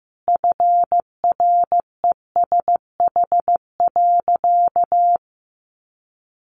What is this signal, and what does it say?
Morse code, heard as a keyed tone, about 15 wpm, 700 Hz. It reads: FRESH.
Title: Sentences from Top 1200 Words 15wpm